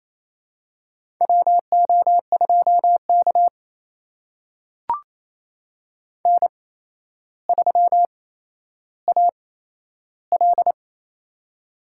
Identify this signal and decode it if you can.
Morse code, heard as a keyed tone, about 28 wpm, spaced as for 12 wpm, 700 Hz. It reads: WO2K N3AL
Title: USA Call Signs - 4 Characters - 8x Character Spacing - 28wpm